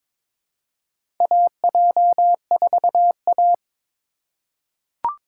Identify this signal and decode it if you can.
Morse code, heard as a keyed tone, about 23 wpm, 700 Hz. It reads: AJ4A
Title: USA Call Signs - 4 Characters - 2x Character Spacing - 22wpm